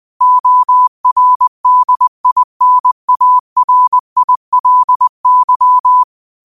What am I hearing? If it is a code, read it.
Morse code, heard as a keyed tone, about 20 wpm, 1000 Hz. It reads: ORDINARILY